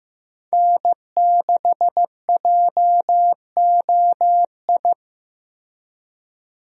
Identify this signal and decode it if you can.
Morse code, heard as a keyed tone, about 15 wpm, 700 Hz. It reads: N6JOI
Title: USA Call Signs - 5 Characters - Repeated 2x - 15wpm